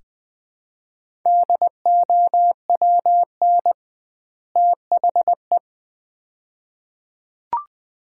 Morse code, heard as a keyed tone, about 21 wpm, 700 Hz. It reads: DOWN THE
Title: 2x Word-Spacing - Sets of 2 Words - 20wpm